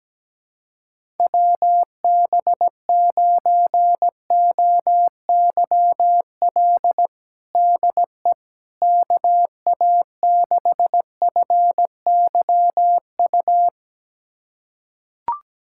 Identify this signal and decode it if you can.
Morse code, heard as a keyed tone, about 17 wpm, 700 Hz. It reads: WB9OYL DE KA6FYU
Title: USA Callsign DE Callsign - 6 Characters - 17wpm